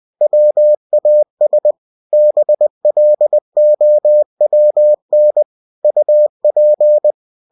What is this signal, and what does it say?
Morse code, heard as a keyed tone, about 20 wpm, 600 Hz. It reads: WAS BLOWN UP
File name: cw-books / winnie-the-pooh / winnie-20wpm-page-0006